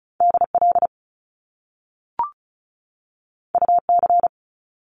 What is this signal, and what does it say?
Morse code, heard as a keyed tone, about 36 wpm, 700 Hz. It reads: DL UC
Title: Top Two Letter Combinations in Words 35wpm